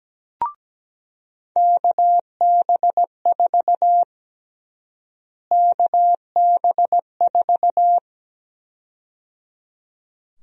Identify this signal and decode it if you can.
Morse code, heard as a keyed tone, about 17 wpm, 700 Hz. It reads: KB4 KB4
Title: USA Call Sign Prefixes - Repeated 2x - 17wpm